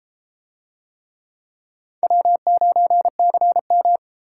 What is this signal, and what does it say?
Morse code, heard as a keyed tone, about 30 wpm, 700 Hz. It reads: W9CM